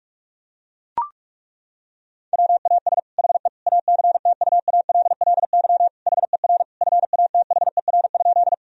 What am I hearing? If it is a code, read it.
Morse code, heard as a keyed tone, about 41 wpm, 700 Hz. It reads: WAS HE ACTUALLY HER FATHER?
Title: Sentences from Top 1000 Words - Review Part 4 - 45wpm